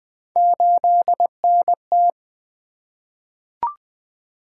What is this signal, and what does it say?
Morse code, heard as a keyed tone, about 20 wpm, 700 Hz. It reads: 8NT
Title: Three Letter-Number Combinations - 20wpm